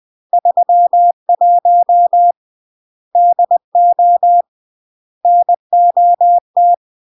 Morse code, heard as a keyed tone, about 20 wpm, 700 Hz. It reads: 31 DO NOT
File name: CW / jodorowsky / 82 commandments 20-20